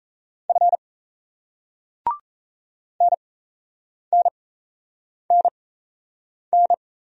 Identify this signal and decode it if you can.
Morse code, heard as a keyed tone, about 38 wpm, spaced as for 8 wpm, 700 Hz. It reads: F NNNN